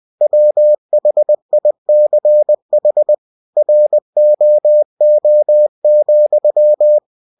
Morse code, heard as a keyed tone, about 20 wpm, 600 Hz. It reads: WHICH ROO,
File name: winnie-20wpm